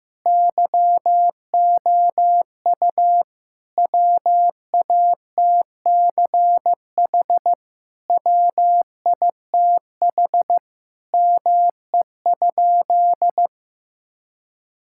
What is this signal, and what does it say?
Morse code, heard as a keyed tone, about 15 wpm, 700 Hz. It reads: YOU WATCH WITH ME?
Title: Sentences from Top 1000 Words - Review Part 5 - 15wpm